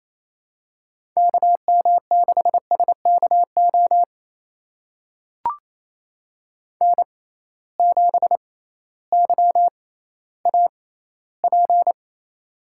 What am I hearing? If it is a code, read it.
Morse code, heard as a keyed tone, about 28 wpm, spaced as for 15 wpm, 700 Hz. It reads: KM6SKO N7YAP
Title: USA Call Signs - 6x Character Spacing - 28wpm